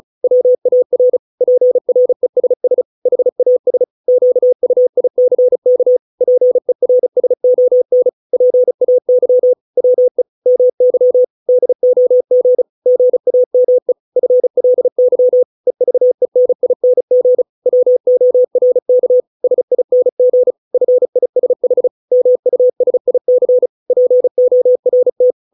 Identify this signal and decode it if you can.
Morse code, heard as a keyed tone, about 34 wpm, 500 Hz. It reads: WAR PRESS HAS QUICK PERSON PAY WE MY DOG GAME FLY EVENING WORK SING FISH MUSIC PORT